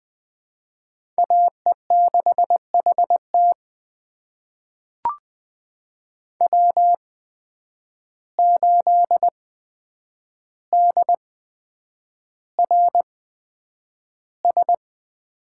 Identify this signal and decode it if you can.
Morse code, heard as a keyed tone, about 20 wpm, spaced as for 9 wpm, 700 Hz. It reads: AE6HT W8DRS
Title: USA Call Signs - 5 Characters - 8x Character Spacing - 20wpm